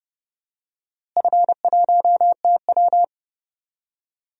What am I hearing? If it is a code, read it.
Morse code, heard as a keyed tone, about 31 wpm, 700 Hz. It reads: F1TW
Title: DXCC-100 Call Signs - 4 Characters - Repeated 2x - 30wpm